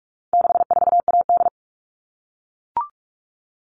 Morse code, heard as a keyed tone, about 44 wpm, 700 Hz. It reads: 64AD
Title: All Cut Numbers - 4-Digit 45wpm